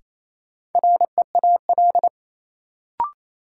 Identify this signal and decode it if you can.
Morse code, heard as a keyed tone, about 28 wpm, 700 Hz. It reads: REAL
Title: Sentences from Top 300 Words - New Word Drill - 28wpm